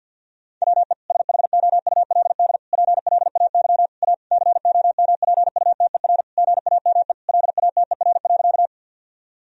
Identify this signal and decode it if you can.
Morse code, heard as a keyed tone, about 50 wpm, 700 Hz. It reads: WE SHOULD PLAY A COMPUTER GAME LATER.
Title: Sentences from Top 1000 Words - Review Part 3 - 50wpm